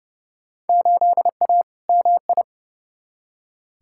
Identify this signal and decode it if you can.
Morse code, heard as a keyed tone, about 32 wpm, 700 Hz. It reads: 8A MI
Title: Field Day Exchanges Encore - 30wpm